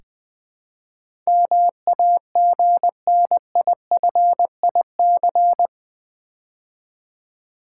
Morse code, heard as a keyed tone, about 20 wpm, 700 Hz. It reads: MAGNIFIC
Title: Build A Word - 4 Letters At A Time - 11-Letter Words - 20wpm